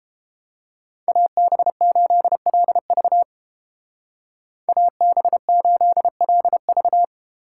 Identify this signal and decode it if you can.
Morse code, heard as a keyed tone, about 28 wpm, 700 Hz. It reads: AB8LV AB8LV